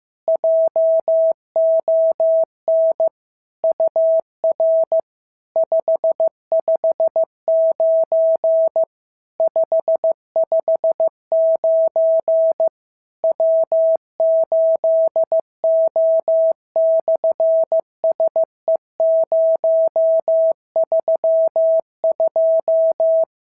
Morse code, heard as a keyed tone, about 15 wpm, 650 Hz. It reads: JON UR 559 559 W8O/SE032
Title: SOTA QSO Practice 15wpm